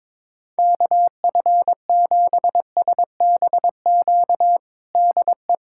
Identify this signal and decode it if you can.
Morse code, heard as a keyed tone, about 22 wpm, 700 Hz. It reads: KF7SBQ DE
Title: USA Callsign DE Callsign 22wpm